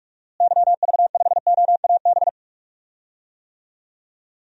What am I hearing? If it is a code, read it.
Morse code, heard as a keyed tone, about 46 wpm, 700 Hz. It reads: YV5OAB
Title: DXCC-100 Call Signs - 6 Characters - Repeated 3x - 45wpm